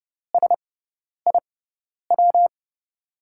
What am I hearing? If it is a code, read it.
Morse code, heard as a keyed tone, about 30 wpm, spaced as for 10 wpm, 700 Hz. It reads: SIW
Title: USA Call Signs - 6x Character Spacing - 30wpm